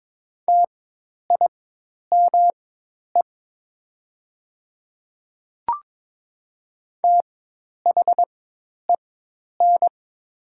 Morse code, heard as a keyed tone, about 22 wpm, spaced as for 10 wpm, 700 Hz. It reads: TIME THEN
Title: 4x Character Spacing - Top 100 Words - 22wpm